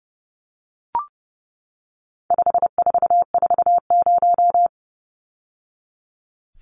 Morse code, heard as a keyed tone, about 30 wpm, 700 Hz. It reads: 5440
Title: Numbers - 4-Digit 30wpm